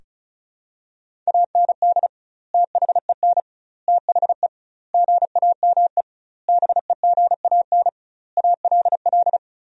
Morse code, heard as a keyed tone, about 34 wpm, 700 Hz. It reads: AND THEN THE GAME BEGAN ALL
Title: 2x Word-Spacing - Sentences from Top 500 Words - 35wpm